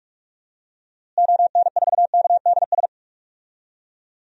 Morse code, heard as a keyed tone, about 41 wpm, 700 Hz. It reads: ON4KDS
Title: DXCC-100 Call Signs - 6 Characters - Repeated 3x - 45wpm